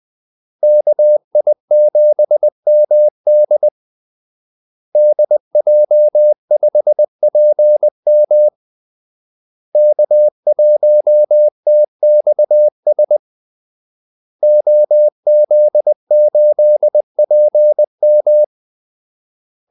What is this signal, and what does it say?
Morse code, heard as a keyed tone, about 20 wpm, 600 Hz. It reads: KI7MD DJ5PM K1TXS OZ8PM
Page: Kaldesignaler 20 wpm